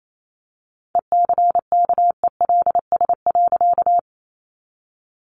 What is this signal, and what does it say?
Morse code, heard as a keyed tone, about 28 wpm, 700 Hz. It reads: ECKELS.
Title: A Sound Of Thunder 28wpm